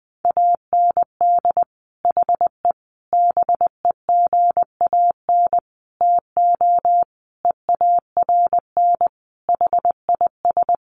Morse code, heard as a keyed tone, about 19 wpm, 700 Hz. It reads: AND HE BEGAN TO EARN HIS
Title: Sentences from Top 1500 Words 20wpm